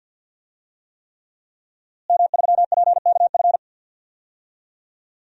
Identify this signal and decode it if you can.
Morse code, heard as a keyed tone, about 51 wpm, 700 Hz. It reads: M3PKF
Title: DXCC-100 Call Signs - 5 Characters - Repeated 3x - 50wpm